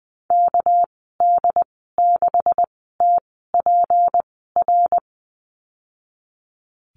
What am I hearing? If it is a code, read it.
Morse code, heard as a keyed tone, about 20 wpm, spaced as for 15 wpm, 700 Hz. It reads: KD6TPR
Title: 213 Lesson 34g - Number 8 - Call Signs 20wpm